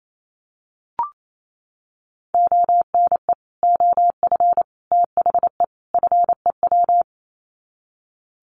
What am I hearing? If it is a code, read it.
Morse code, heard as a keyed tone, about 28 wpm, 700 Hz. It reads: ONE OF THE FEW
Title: Sets of 4 Words 28wpm